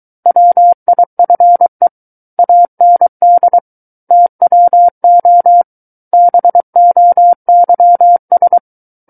Morse code, heard as a keyed tone, about 23 wpm, 700 Hz. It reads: WIFE AND TWO BOYS
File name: Book One / 14-Chapter 14 - 23 WPM